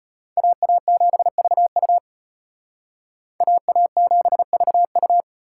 Morse code, heard as a keyed tone, about 32 wpm, 700 Hz. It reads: AA7VU AA7VU